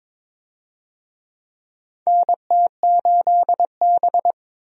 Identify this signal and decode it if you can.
Morse code, heard as a keyed tone, about 22 wpm, 700 Hz. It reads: NT8B
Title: USA Call Signs - 4 Characters - 22wpm